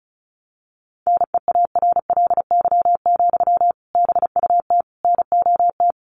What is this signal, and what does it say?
Morse code, heard as a keyed tone, about 34 wpm, 700 Hz. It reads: NEARLY, BUT NOT
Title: Sentences from Top 1300 Words 35wpm